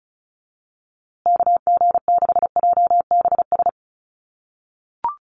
Morse code, heard as a keyed tone, about 34 wpm, 700 Hz. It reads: KG6JBS
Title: USA Call Signs Repeated 3x 35wpm